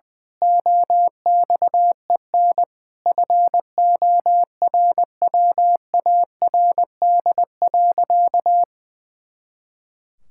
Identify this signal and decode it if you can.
Morse code, heard as a keyed tone, about 20 wpm, 700 Hz. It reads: OXEN FORWARD.
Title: Pangram Challenge v2 - 20wpm